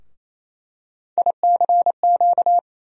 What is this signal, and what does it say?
Morse code, heard as a keyed tone, about 28 wpm, 700 Hz. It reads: ICQ
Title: USA Call Sign Postfixes - 3 Letters 28wpm